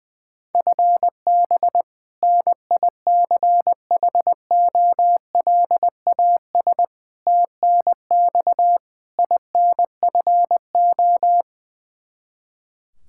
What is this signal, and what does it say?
Morse code, heard as a keyed tone, about 21 wpm, 700 Hz. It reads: FB NICHOLAS TNX INFO
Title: K1USN SST QSO Practice 20wpm